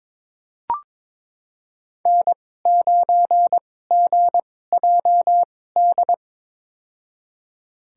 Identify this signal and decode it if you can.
Morse code, heard as a keyed tone, about 22 wpm, spaced as for 16 wpm, 700 Hz. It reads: N9GJD
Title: USA Call Signs - 5 Characters - 2x Character Spacing - 22wpm